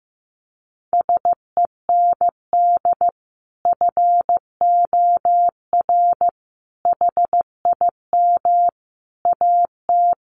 Morse code, heard as a keyed tone, about 15 wpm, 700 Hz. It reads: SEND FOR HIM AT
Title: Sentences from Top 1500 Words 15wpm